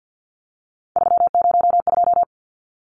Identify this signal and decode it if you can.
Morse code, heard as a keyed tone, about 50 wpm, 700 Hz. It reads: 302
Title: Numbers - 3-Digit 50wpm